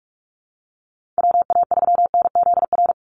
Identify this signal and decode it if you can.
Morse code, heard as a keyed tone, about 44 wpm, 700 Hz. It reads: WA3NZR
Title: Call Signs Repeated 3x - 45wpm